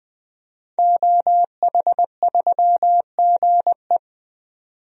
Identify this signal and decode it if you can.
Morse code, heard as a keyed tone, about 20 wpm, 700 Hz. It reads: OH3GE
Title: DXCC-100 Call Signs - 5 Characters - 20wpm